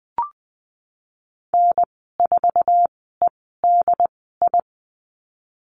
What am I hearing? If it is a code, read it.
Morse code, heard as a keyed tone, about 20 wpm, spaced as for 15 wpm, 700 Hz. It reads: N4EDI
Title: 069 Lesson 12g - Letter d - Call Signs 20wpm